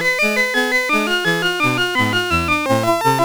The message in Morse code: none